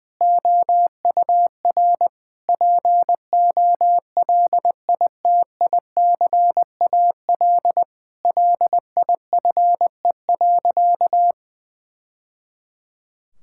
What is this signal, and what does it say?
Morse code, heard as a keyed tone, about 20 wpm, 700 Hz. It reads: OUR POLITICAL LIFE.
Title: Sentences from Top 1000 Words - Review Part 5 - 20wpm